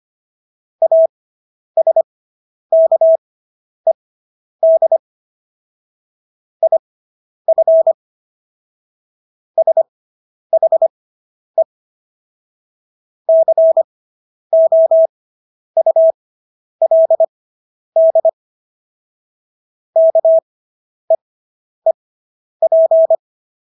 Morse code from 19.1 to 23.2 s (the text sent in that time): KEEP